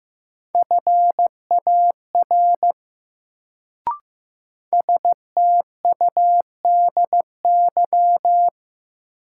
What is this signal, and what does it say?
Morse code, heard as a keyed tone, about 15 wpm, 700 Hz. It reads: FAR STUDY
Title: Sentences from Top 300 Words - New Word Drill - 15wpm